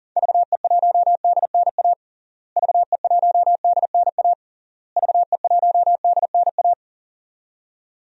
Morse code, heard as a keyed tone, about 40 wpm, 700 Hz. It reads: VE1DNA VE1DNA VE1DNA
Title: DXCC-100 Call Signs - 6 Characters - Repeated 3x - 40wpm